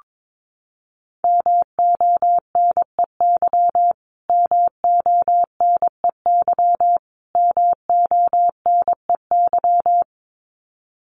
Morse code, heard as a keyed tone, about 22 wpm, 700 Hz. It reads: MONEY MONEY MONEY
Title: Top 500 Words Repeated 22wpm